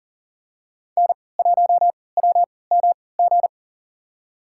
Morse code, heard as a keyed tone, about 38 wpm, spaced as for 22 wpm, 700 Hz. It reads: N1WMG
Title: USA Call Signs - 3x Character Spacing - 40wpm